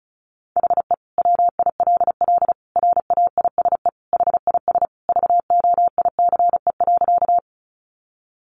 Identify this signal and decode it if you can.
Morse code, heard as a keyed tone, about 34 wpm, 700 Hz. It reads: HE WILL RAISE HIS VOICE.
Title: Sentences from Top 1400 Words 35wpm